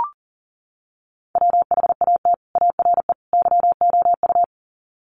Morse code, heard as a keyed tone, about 40 wpm, 700 Hz. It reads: WHAT ARE YOU
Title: Sets of 3 Words 40wpm